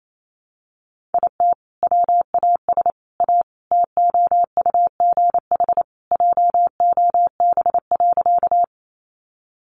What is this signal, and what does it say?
Morse code, heard as a keyed tone, about 28 wpm, 700 Hz. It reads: IT WAS A TOUGH JOB.
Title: Sentences from Top 1200 Words 28wpm